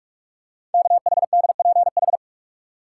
Morse code, heard as a keyed tone, about 44 wpm, 700 Hz. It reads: KHDPH
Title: Code Groups - 5 Letters - 45wpm